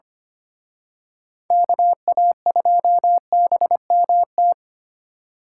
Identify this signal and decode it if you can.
Morse code, heard as a keyed tone, about 29 wpm, 700 Hz. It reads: KA2BMT
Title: Speed Racing - USA 2x3 Call Signs 25wpm